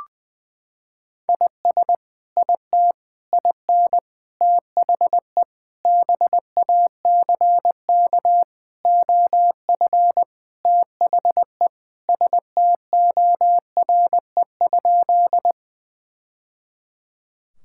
Morse code, heard as a keyed tone, about 21 wpm, 700 Hz. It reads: IS IT IN THE BACK OF THE STORE?
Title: Sentences from Top 1000 Words - Review Part 5 - 20wpm